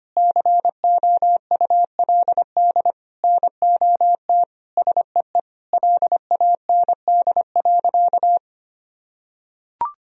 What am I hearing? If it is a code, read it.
Morse code, heard as a keyed tone, about 25 wpm, 700 Hz. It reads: COULD NOT SEE LAND.
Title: Sentences from Top 1000 Words - Review Part 2 - 25wpm